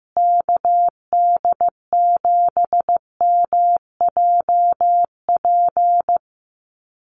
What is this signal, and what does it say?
Morse code, heard as a keyed tone, about 15 wpm, 700 Hz. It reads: KD7MJP
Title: USA Call Signs Repeated 3x 15wpm